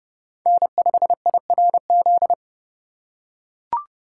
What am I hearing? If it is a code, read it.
Morse code, heard as a keyed tone, about 30 wpm, 700 Hz. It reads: N5IRZ
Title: USA Call Signs - 5 Characters - 3x Character Spacing - 30wpm